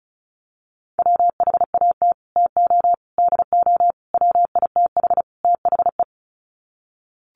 Morse code, heard as a keyed tone, about 34 wpm, 700 Hz. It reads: WHAT TO DO WITH THE
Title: Sets of 5 Words Encore2 35wpm